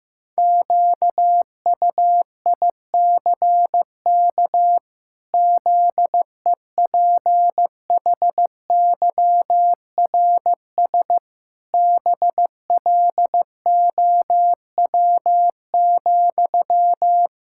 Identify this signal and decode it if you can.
Morse code, heard as a keyed tone, about 15 wpm, 700 Hz. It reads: QUICK ZEPHYRS BLOW,
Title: Pangram Challenge v2 - 15wpm